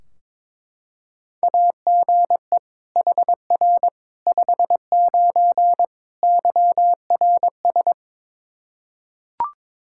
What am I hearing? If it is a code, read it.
Morse code, heard as a keyed tone, about 22 wpm, 700 Hz. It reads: AGE HR 59 YRS
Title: K1USN SST QSO Practice 22wpm